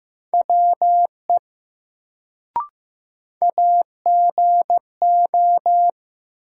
Morse code, heard as a keyed tone, about 15 wpm, 700 Hz. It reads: WE AGO
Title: Sets of 2 Words - New Word Drill - 15wpm